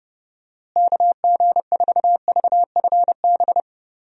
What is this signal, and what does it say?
Morse code, heard as a keyed tone, about 30 wpm, 700 Hz. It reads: KG4VFB
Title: USA Call Signs - 6 Characters - Repeated 2x - 30wpm